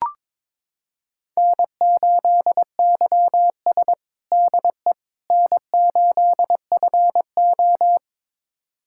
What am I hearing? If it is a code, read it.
Morse code, heard as a keyed tone, about 22 wpm, 700 Hz. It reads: N8YS DE N8FO
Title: USA Callsign DE Callsign - 4 Characters - 22wpm